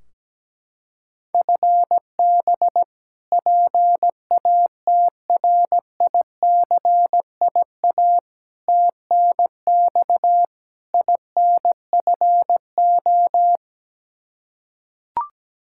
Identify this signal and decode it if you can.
Morse code, heard as a keyed tone, about 17 wpm, 700 Hz. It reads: FB PATRICIA TNX INFO
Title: Practice QSO - Exchange 3 - More Info - 17wpm